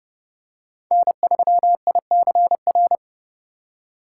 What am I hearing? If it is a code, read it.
Morse code, heard as a keyed tone, about 29 wpm, 700 Hz. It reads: N3ICR